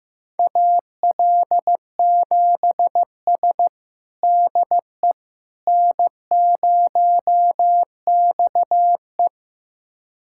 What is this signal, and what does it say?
Morse code, heard as a keyed tone, about 15 wpm, 700 Hz. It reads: AL7S DE N0XE
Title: USA Callsign DE Callsign - 4 Characters - 15wpm